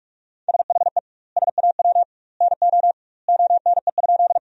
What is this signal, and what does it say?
Morse code, heard as a keyed tone, about 42 wpm, 700 Hz. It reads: SHE SAW NO ONE?
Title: 2x Word-Spacing - Sentences from Top 300 Words - 45wpm